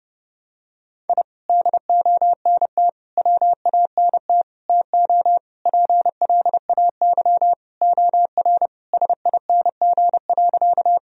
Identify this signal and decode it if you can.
Morse code, heard as a keyed tone, about 32 wpm, 700 Hz. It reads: I DONT WANT TO PLAY OR SING.
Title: Sentences from Top 1000 Words - Review Part 2 - 30wpm